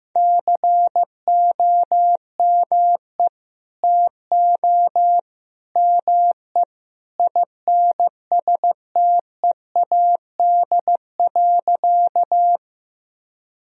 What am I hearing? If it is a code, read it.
Morse code, heard as a keyed tone, about 15 wpm, 700 Hz. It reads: COME TO ME INSTEAD.
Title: Sentences from Top 1000 Words - Review Part 4 - 15wpm